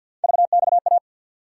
Morse code, heard as a keyed tone, about 53 wpm, 700 Hz. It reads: VXA
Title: USA Call Sign Postfixes - 3 Letters 50wpm